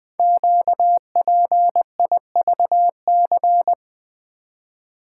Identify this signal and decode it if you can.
Morse code, heard as a keyed tone, about 20 wpm, 700 Hz. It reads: QPIVC
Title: Code Groups - 5 Letters - 20wpm